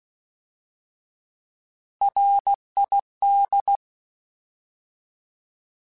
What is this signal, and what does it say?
Morse code, heard as a keyed tone, about 16 wpm, 800 Hz. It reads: RID
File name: Alphabetical / Alphabetical mp3s / Rs @15